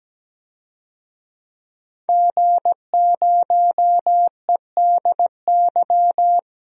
Morse code, heard as a keyed tone, about 17 wpm, 700 Hz. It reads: G0EDY
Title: DXCC-100 Call Signs - 5 Characters - 17wpm